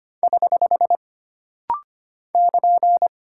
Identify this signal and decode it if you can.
Morse code, heard as a keyed tone, about 25 wpm, 700 Hz. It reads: <HH> <KN>
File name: Single Character (Letter:Number:Punctuation:Prosign) - Rapid-Fire - 25wpm